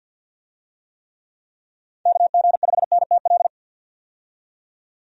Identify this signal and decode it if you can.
Morse code, heard as a keyed tone, about 47 wpm, 700 Hz. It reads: KG5NTL
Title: USA Call Signs - 6 Characters - 50wpm